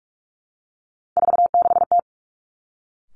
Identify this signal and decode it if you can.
Morse code, heard as a keyed tone, about 46 wpm, 700 Hz. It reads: 46T
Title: Three Challenging Letter-Number Combinations 45wpm